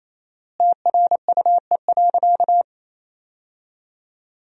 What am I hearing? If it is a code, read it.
Morse code, heard as a keyed tone, about 28 wpm, 700 Hz. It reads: TRUE.
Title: 4x Word-Spacing - Sentences from Top 400 Words - 28wpm